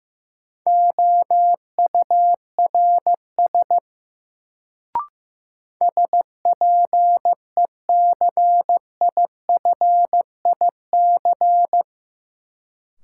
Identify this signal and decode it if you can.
Morse code, heard as a keyed tone, about 15 wpm, 700 Hz. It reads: OURS SPECIFIC
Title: Sentences from Top 1300 Words - New Word Drill Alt - 15wpm